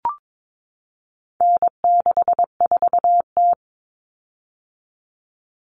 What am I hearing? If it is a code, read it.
Morse code, heard as a keyed tone, about 22 wpm, 700 Hz. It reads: N64T
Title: Common Cut Numbers - 4-Digit 22wpm